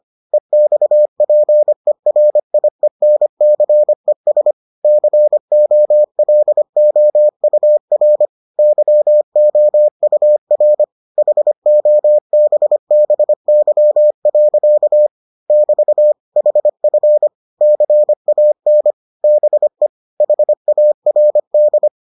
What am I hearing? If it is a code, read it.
Morse code, heard as a keyed tone, about 25 wpm, 600 Hz. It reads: EXPERIENCES COLOUR YOUR HOBBY. <BT> HF CAN BE HARD